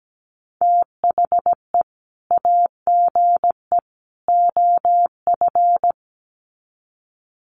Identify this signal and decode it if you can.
Morse code, heard as a keyed tone, about 17 wpm, 700 Hz. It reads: THE AGE OF
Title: Sets of 3 Words 17wpm